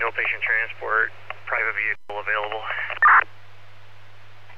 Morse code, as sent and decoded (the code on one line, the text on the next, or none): none